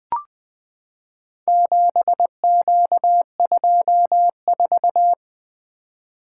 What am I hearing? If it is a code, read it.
Morse code, heard as a keyed tone, about 20 wpm, 700 Hz. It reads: 7Q24
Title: Four Letter-Number Combinations - 20wpm